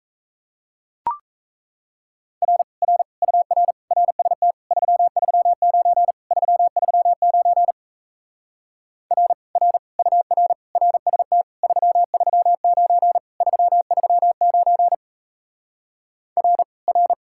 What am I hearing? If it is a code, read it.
Morse code, heard as a keyed tone, about 38 wpm, 700 Hz. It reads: R R UR RST 339 339 R R UR RST 339 339 R R